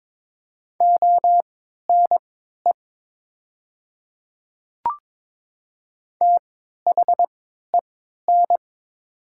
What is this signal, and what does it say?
Morse code, heard as a keyed tone, about 22 wpm, spaced as for 12 wpm, 700 Hz. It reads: ONE THEN
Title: 3x Character Spacing - Top 100 Words - 22wpm